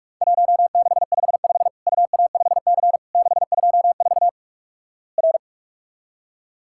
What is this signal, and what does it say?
Morse code, heard as a keyed tone, about 46 wpm, 700 Hz. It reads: 1655 UA5C 624 R
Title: Worked All Europe DX Contest - QTC Exchange - 45wpm